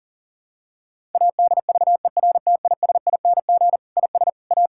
Morse code, heard as a keyed tone, about 40 wpm, 700 Hz. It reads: ADVERTISING IS A
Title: Sentences from Top 1000 Words - Review Part 6 - 40wpm